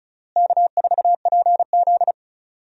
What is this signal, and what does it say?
Morse code, heard as a keyed tone, about 36 wpm, 700 Hz. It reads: K4PZ
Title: USA Call Signs - 4 Characters - Repeated 3x - 35wpm